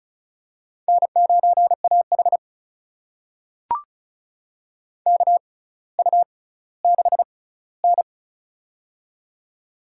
Morse code, heard as a keyed tone, about 36 wpm, spaced as for 18 wpm, 700 Hz. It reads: N9AH KU6N
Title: USA Call Signs - 4 Characters - 6x Character Spacing - 35wpm